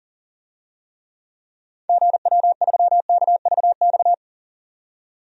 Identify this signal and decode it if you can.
Morse code, heard as a keyed tone, about 40 wpm, 700 Hz. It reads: GW3KVX
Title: DXCC-100 Call Signs - 6 Characters - 40wpm